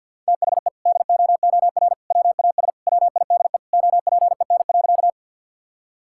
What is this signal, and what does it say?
Morse code, heard as a keyed tone, about 48 wpm, 700 Hz. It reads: THE DOOR WAS WIDE OPEN.
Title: Sentences from Top 1000 Words - Review Part 6 - 50wpm